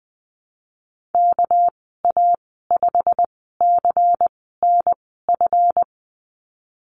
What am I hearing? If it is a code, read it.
Morse code, heard as a keyed tone, about 20 wpm, spaced as for 15 wpm, 700 Hz. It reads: KA5CNF
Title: 207 Lesson 33g - Letter j - Call Signs 20wpm